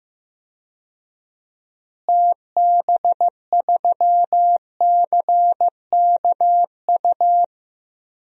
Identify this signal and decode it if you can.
Morse code, heard as a keyed tone, about 15 wpm, 700 Hz. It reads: TB3CKU
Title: DXCC-100 Call Signs - 6 Characters - Repeated 2x - 15wpm